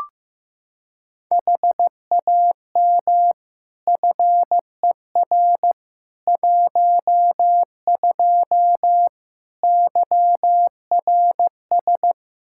Morse code, heard as a keyed tone, about 15 wpm, 700 Hz. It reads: HAM FER 12 YRS
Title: Practice QSO - Exchange 3 - More Info - 15wpm